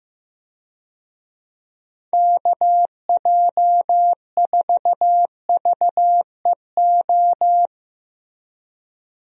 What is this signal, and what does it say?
Morse code, heard as a keyed tone, about 16 wpm, 700 Hz. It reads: KJ4VEO
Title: USA Call Signs - 6 Characters - 15wpm